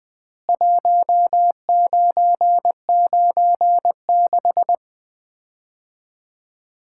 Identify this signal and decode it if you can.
Morse code, heard as a keyed tone, about 20 wpm, 700 Hz. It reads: 1996
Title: Common QSO Phrases - 4x Word-Spacing - 20wpm